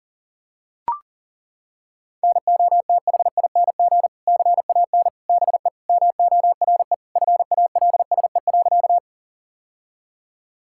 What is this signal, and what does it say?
Morse code, heard as a keyed tone, about 40 wpm, 700 Hz. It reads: NOTHING CAN BE MORE FALSE.
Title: Sentences from Top 1600 Words 40wpm